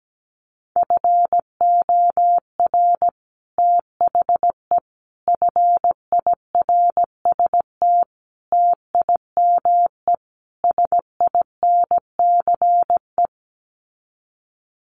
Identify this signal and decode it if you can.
Morse code, heard as a keyed tone, about 17 wpm, 700 Hz. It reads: FOR THE FIRST TIME SINCE
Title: Sets of 5 Words 17wpm